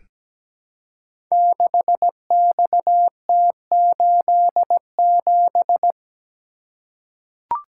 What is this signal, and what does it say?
Morse code, heard as a keyed tone, about 17 wpm, 690 Hz. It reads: 6XT87
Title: Code Groups - 5 Letter:Numbers - 17wpm